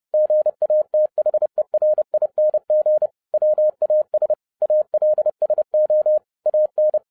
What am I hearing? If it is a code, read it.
Morse code, heard as a keyed tone, about 30 wpm, 610 Hz. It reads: GATHERING WAS ALSO AN